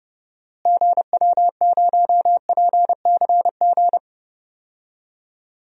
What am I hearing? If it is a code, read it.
Morse code, heard as a keyed tone, about 30 wpm, 700 Hz. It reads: GW0PCG
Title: DXCC-100 Call Signs - 6 Characters - Repeated 3x - 30wpm